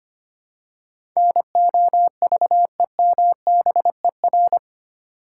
Morse code, heard as a keyed tone, about 25 wpm, 700 Hz. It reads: NOVEMBER
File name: 8 Letter Words - First 3 Letters Given - 25wpm